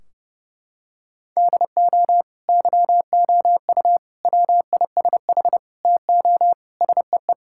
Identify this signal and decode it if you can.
Morse code, heard as a keyed tone, about 29 wpm, 700 Hz. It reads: DO YOU WISH TO SEE
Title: Sentences from Top 1600 Words 30wpm